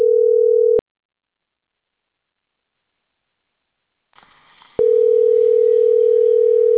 Telephone call progress tone (North America), heard as a telephone network's tone, about 440, 480 Hz. Ringback tone.